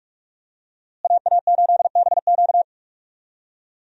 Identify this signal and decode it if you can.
Morse code, heard as a keyed tone, about 46 wpm, 700 Hz. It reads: AA8BQ
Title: USA Call Signs - 5 Characters - Repeated 3x - 45wpm